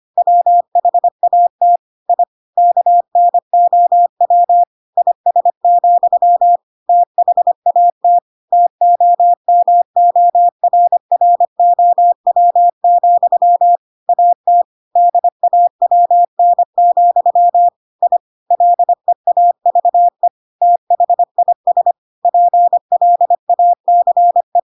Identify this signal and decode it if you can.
Morse code, heard as a keyed tone, about 25 wpm, 700 Hz. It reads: WHAT I KNOW IS, THAT TOMORROW, AT DAWN, I LEAVE THIS PLACE
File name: Pinocchio / Pinocchio ch4 25 25 ik1hge